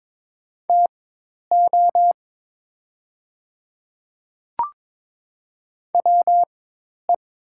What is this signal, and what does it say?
Morse code, heard as a keyed tone, about 22 wpm, spaced as for 10 wpm, 700 Hz. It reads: TO WE
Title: 4x Character Spacing - Top 300 Words - 22wpm